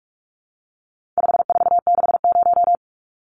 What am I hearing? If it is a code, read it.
Morse code, heard as a keyed tone, about 44 wpm, 700 Hz. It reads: 5460
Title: Numbers - 4-Digit 45wpm